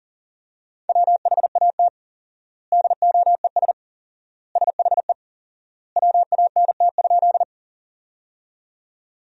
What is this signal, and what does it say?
Morse code, heard as a keyed tone, about 40 wpm, 700 Hz. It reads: WHAT DOES SHE WANT?
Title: 4x Word-Spacing - Sentences from Top 200 Words - 40wpm